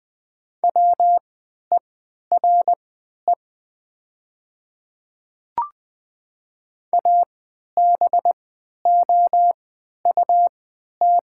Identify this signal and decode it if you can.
Morse code, heard as a keyed tone, about 20 wpm, spaced as for 11 wpm, 700 Hz. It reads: WERE ABOUT
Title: 3x Character Spacing - Top 200 Words - 20wpm